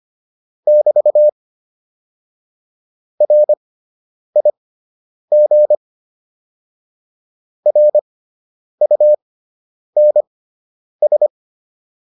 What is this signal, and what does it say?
Morse code, heard as a keyed tone, about 25 wpm, spaced as for 9 wpm, 600 Hz. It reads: <BT> RIG RUNS
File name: QSO 202 09